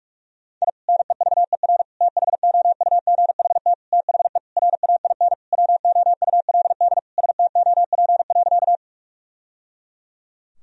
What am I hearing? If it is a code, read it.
Morse code, heard as a keyed tone, about 46 wpm, 700 Hz. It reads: I NEVER THOUGHT THE RAIN WOULD STOP.
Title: Sentences from Top 1000 Words - Review Part 1 - 45wpm